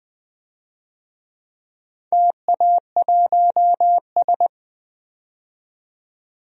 Morse code, heard as a keyed tone, about 20 wpm, 700 Hz. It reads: TA1S
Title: DXCC-100 Call Signs - 4 Characters - 20wpm